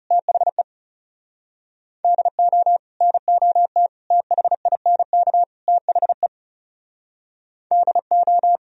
Morse code, heard as a keyed tone, about 39 wpm, 700 Hz. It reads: THE DO NOT THINK THE DO